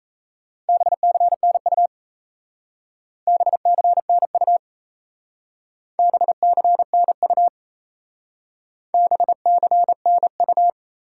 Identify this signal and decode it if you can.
Morse code, heard as a keyed tone, about 35 wpm, 700 Hz. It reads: BCNU BCNU BCNU BCNU
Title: Speed-Racing Top 180 QSO Elements 28wpm